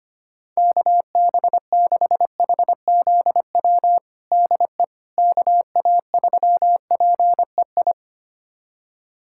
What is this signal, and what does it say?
Morse code, heard as a keyed tone, about 25 wpm, 700 Hz. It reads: KB6HZW DE KA3PEI
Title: USA Callsign DE Callsign - 6 Characters - 25wpm